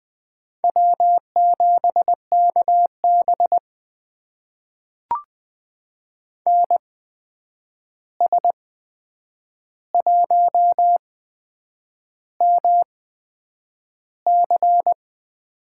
Morse code, heard as a keyed tone, about 21 wpm, spaced as for 8 wpm, 700 Hz. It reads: W7KB NS1MC
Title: USA Call Signs - 8x Character Spacing - 20wpm